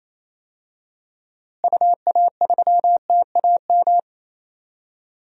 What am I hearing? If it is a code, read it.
Morse code, heard as a keyed tone, about 27 wpm, 700 Hz. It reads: UA3TAM
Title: DXCC-100 Call Signs - 6 Characters - Repeated 2x - 28wpm